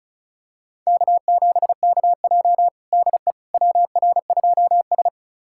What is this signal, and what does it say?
Morse code, heard as a keyed tone, about 34 wpm, 700 Hz. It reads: K7KJ DE WR2S
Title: USA Callsign DE Callsign - 4 Characters - 35wpm